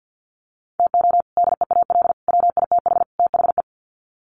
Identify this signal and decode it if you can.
Morse code, heard as a keyed tone, about 50 wpm, 700 Hz. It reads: TO DEAL WITH THE
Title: Sets of 4 Words 50wpm